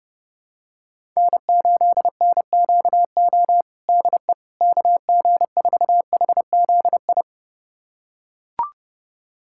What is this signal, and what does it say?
Morse code, heard as a keyed tone, about 30 wpm, 700 Hz. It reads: N8NQO DE KG4HZI
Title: USA Callsign DE Callsign 30wpm